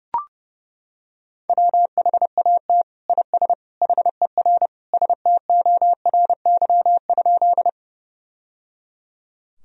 Morse code, heard as a keyed tone, about 30 wpm, 700 Hz. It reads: WHAT IS HER STORY?
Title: Sentences from Top 1000 Words - Review Part 1 - 30wpm